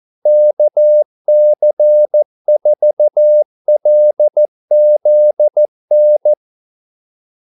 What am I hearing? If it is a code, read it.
Morse code, heard as a keyed tone, about 14 wpm, 600 Hz. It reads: KC4LZN
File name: Callsigns-14wpm